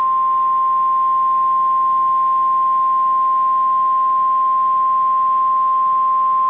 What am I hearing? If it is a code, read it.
Emergency Alert System (SAME) tones (the 1050 Hz attention tone). It is heard from 0.9 to 5.7 s.
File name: EAS Logs / KMNR